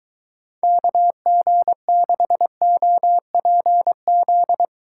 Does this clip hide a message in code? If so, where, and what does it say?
Yes: Morse code from 0.6 to 4.7 s, heard as a keyed tone, about 23 wpm, 700 Hz. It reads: KG6OPZ